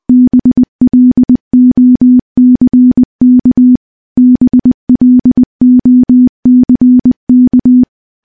Morse code, heard as a keyed tone, about 21 wpm, 250 Hz. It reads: BLOCK BLOCK
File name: morse / morse-training-20WPM